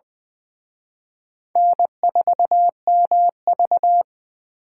Morse code, heard as a keyed tone, about 23 wpm, 700 Hz. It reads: N4MV
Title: Speed Racing - USA Call Signs 20wpm